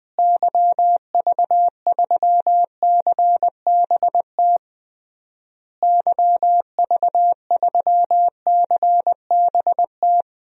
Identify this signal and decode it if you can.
Morse code, heard as a keyed tone, about 20 wpm, 700 Hz. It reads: YV3CBT YV3CBT
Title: DXCC-100 Call Signs - 6 Characters - Repeated 3x - 20wpm